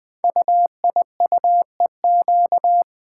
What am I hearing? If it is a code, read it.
Morse code, heard as a keyed tone, about 20 wpm, 700 Hz. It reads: UIUEQ
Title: Code Groups - 5 Letters - 20wpm